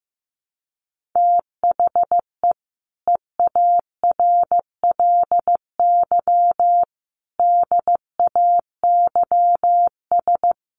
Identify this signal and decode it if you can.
Morse code, heard as a keyed tone, about 15 wpm, 700 Hz. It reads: THE EARLY DAYS
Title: Sets of 5 Words Encore 15wpm